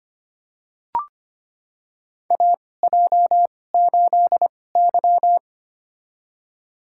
Morse code, heard as a keyed tone, about 25 wpm, spaced as for 18 wpm, 700 Hz. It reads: AJ8Y
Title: USA Call Signs - 4 Characters - 2x Character Spacing - 25wpm